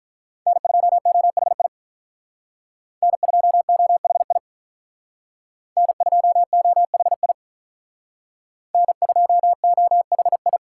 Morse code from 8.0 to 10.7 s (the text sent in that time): N2OHI